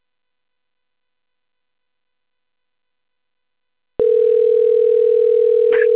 Ringback tone (North America), 440 + 480 Hz.